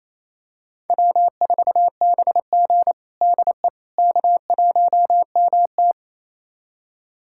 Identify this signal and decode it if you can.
Morse code, heard as a keyed tone, about 28 wpm, 700 Hz. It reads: W4BG DE K1MT
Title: USA Callsign DE Callsign - 4 Characters - 28wpm